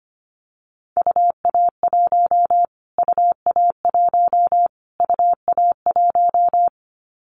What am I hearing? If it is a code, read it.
Morse code, heard as a keyed tone, about 25 wpm, 700 Hz. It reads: UA1 UA1 UA1
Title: Callsign Prefixes for Top 25 DXCC - Repeated 3x - 25wpm